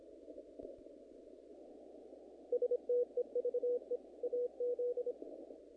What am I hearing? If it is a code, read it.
Morse code, heard as a keyed tone, about 24 wpm, 470 Hz. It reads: STEVE AZ
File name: contests / naqp-cw-k5tr-op-k5pi- -1900